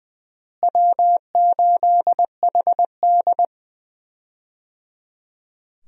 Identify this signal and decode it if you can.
Morse code, heard as a keyed tone, about 20 wpm, 700 Hz. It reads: W8HD
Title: USA Call Signs - 4 Characters - Repeated 3x - 20wpm